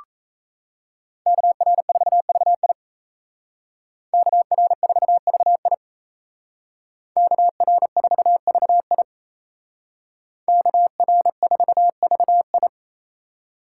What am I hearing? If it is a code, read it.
Morse code, heard as a keyed tone, about 36 wpm, 700 Hz. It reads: KR4VI KR4VI KR4VI KR4VI